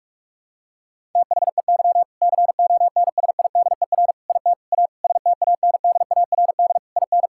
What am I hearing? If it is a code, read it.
Morse code, heard as a keyed tone, about 41 wpm, 700 Hz. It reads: THEY CONSIDER IT A STANDARD IN
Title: Sentences from Top 1000 Words - Review Part 6 - 45wpm